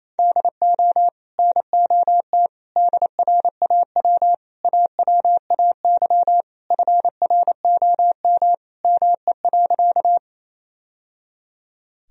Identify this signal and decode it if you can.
Morse code, heard as a keyed tone, about 28 wpm, 700 Hz. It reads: DO NOT DRAW AWAY FROM ME.
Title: Sentences from Top 1000 Words - Review Part 1 - 28wpm